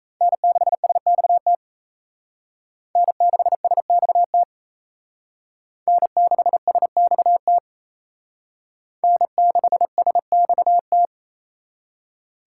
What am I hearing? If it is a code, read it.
Morse code, heard as a keyed tone, about 32 wpm, 700 Hz. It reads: N6SXT N6SXT N6SXT N6SXT
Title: Speed Racing - USA Call Signs 28wpm